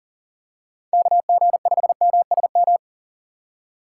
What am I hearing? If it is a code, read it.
Morse code, heard as a keyed tone, about 40 wpm, 700 Hz. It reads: KG5MSM
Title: USA Call Signs - 6 Characters - 3x Character Spacing - 40wpm